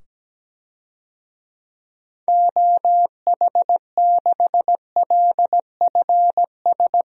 Morse code, heard as a keyed tone, about 17 wpm, 700 Hz. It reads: OH6LFS
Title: DXCC-100 Call Signs - 6 Characters - 17wpm